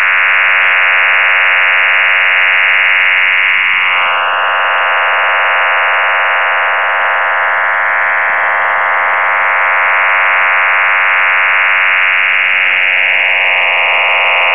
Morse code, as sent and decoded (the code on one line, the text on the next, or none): none